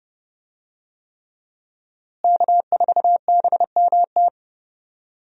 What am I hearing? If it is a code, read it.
Morse code, heard as a keyed tone, about 30 wpm, 700 Hz. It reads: K4BMT
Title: USA Call Signs - 5 Characters - Repeated 3x - 30wpm